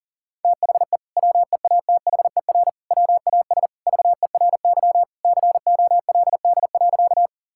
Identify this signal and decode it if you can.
Morse code, heard as a keyed tone, about 40 wpm, 700 Hz. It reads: THE WEATHER WAS VERY COLD.
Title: Sentences from Top 1000 Words - Review Part 5 - 40wpm